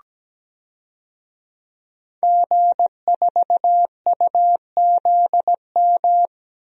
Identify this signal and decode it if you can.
Morse code, heard as a keyed tone, about 17 wpm, 700 Hz. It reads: G4UZM
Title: DXCC-100 Call Signs - 5 Characters - 17wpm